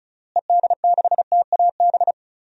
Morse code, heard as a keyed tone, about 36 wpm, 700 Hz. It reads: ED6TAB
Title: DXCC-100 Call Signs - 6 Characters - Repeated 3x - 35wpm